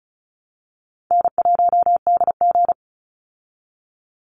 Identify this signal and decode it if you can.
Morse code, heard as a keyed tone, about 34 wpm, 700 Hz. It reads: N1DG
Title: Call Signs Repeated 2x - 35wpm